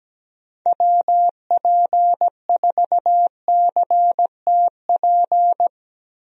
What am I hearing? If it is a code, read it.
Morse code, heard as a keyed tone, about 17 wpm, 700 Hz. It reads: WP4CTP
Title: USA Call Signs - 6 Characters - 17wpm